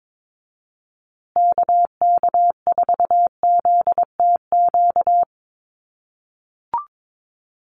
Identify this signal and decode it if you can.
Morse code, heard as a keyed tone, about 22 wpm, 690 Hz. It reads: KK4ZTQ
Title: USA Call Signs Repeated 3x 22wpm